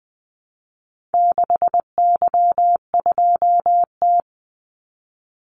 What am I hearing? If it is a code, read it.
Morse code, heard as a keyed tone, about 20 wpm, 700 Hz. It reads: 6Y2T
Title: Call Signs Repeated 3x - 20wpm